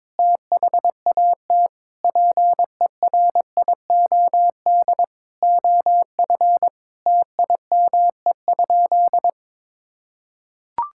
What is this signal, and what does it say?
Morse code, heard as a keyed tone, about 22 wpm, 700 Hz. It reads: THAT PERIOD OF TIME?
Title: Sentences from Top 1000 Words - Review Part 5 - 22wpm